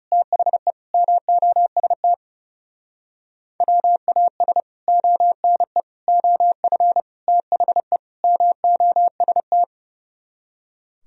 Morse code, heard as a keyed tone, about 40 wpm, 700 Hz. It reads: THE MOST WAS ONE OF THE MOST